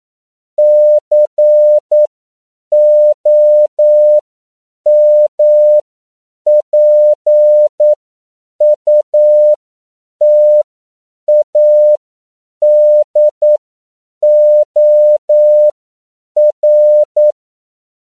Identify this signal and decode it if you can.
Morse code, heard as a keyed tone, about 9 wpm, spaced as for 7 wpm, 600 Hz. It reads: COMPUTADOR